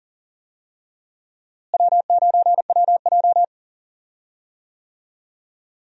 Morse code, heard as a keyed tone, about 40 wpm, 700 Hz. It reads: W9WJ
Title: USA Call Signs - 4 Characters - Repeated 3x - 40wpm